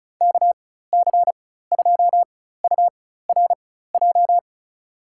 Morse code, heard as a keyed tone, about 36 wpm, spaced as for 16 wpm, 700 Hz. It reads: KC2URJ
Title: USA Call Signs - 6 Characters - 4x Character Spacing - 35wpm